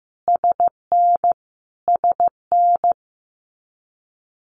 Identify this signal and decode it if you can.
Morse code, heard as a keyed tone, about 15 wpm, 700 Hz. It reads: SN SN
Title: Abbreviations Repeated 3x - 15wpm